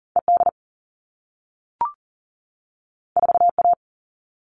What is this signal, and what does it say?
Morse code, heard as a keyed tone, about 41 wpm, 700 Hz. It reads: ED 4A
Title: All Cut Numbers - 2-Digit 40wpm